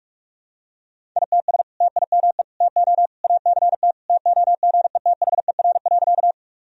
Morse code, heard as a keyed tone, about 46 wpm, 700 Hz. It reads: ITS TIME TO ACT TOGETHER.
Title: Sentences from Top 1000 Words - Review Part 1 - 45wpm